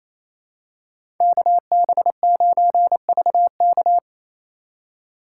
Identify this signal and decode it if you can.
Morse code, heard as a keyed tone, about 27 wpm, 700 Hz. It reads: KB9VK
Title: USA Call Signs - 4x Character Spacing - 28wpm